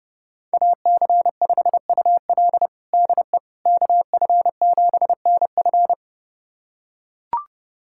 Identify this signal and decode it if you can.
Morse code, heard as a keyed tone, about 30 wpm, 700 Hz. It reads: AC5UL DE KF7NF
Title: USA Callsign DE Callsign 30wpm